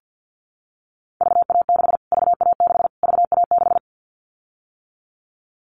Morse code, heard as a keyed tone, about 50 wpm, 700 Hz. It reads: VA6 VA6 VA6
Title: Callsign Prefixes for Top 25 DXCC - Repeated 3x - 50wpm